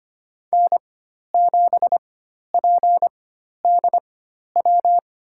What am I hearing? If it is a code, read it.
Morse code, heard as a keyed tone, about 25 wpm, spaced as for 12 wpm, 700 Hz. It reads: N7PDW
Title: USA Call Signs - 5 Characters - 4x Character Spacing - 25wpm